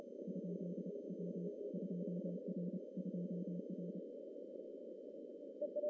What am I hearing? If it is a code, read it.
Morse code, heard as a keyed tone, about 29 wpm, 180 Hz. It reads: ? W 2R 2R
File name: CW / cqww-cw-k5tr